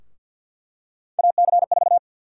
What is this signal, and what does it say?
Morse code, heard as a keyed tone, about 51 wpm, 700 Hz. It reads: AC4
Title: USA Call Sign Prefixes - 50wpm